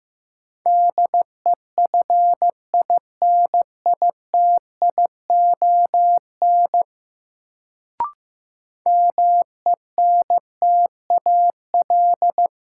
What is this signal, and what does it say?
Morse code, heard as a keyed tone, about 15 wpm, 700 Hz. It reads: DEFINITION MENTAL
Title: Sentences from Top 1000 Words - Review Part 4 - New Word Drill - 15wpm